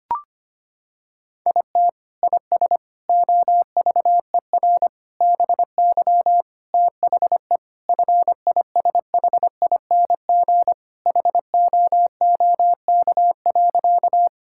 Morse code, heard as a keyed tone, about 25 wpm, 700 Hz. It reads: IT IS OVER BY THE FISHING HOOK.
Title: Sentences from Top 1600 Words 25wpm